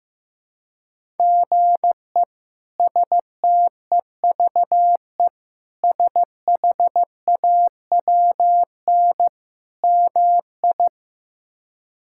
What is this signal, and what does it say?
Morse code, heard as a keyed tone, about 15 wpm, 700 Hz. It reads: GE STEVE SHAWN MI
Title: K1USN SST QSO Practice - 15wpm